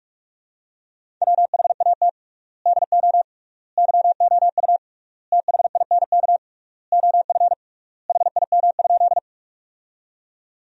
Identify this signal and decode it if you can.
Morse code, heard as a keyed tone, about 41 wpm, 700 Hz. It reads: WHAT DO YOU THINK OF HIM?
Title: 3x Word-Spacing - Sentences from Top 200 Words - 45wpm